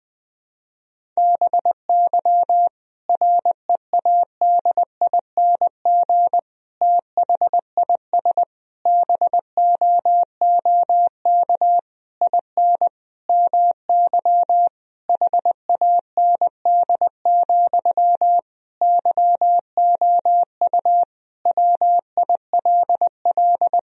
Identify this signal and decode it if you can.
Morse code, heard as a keyed tone, about 20 wpm, 700 Hz. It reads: BY READING THIS BOOK IN MY HAND, YOU WILL
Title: Sentences from Top 1000 Words - Review Part 3 - 20wpm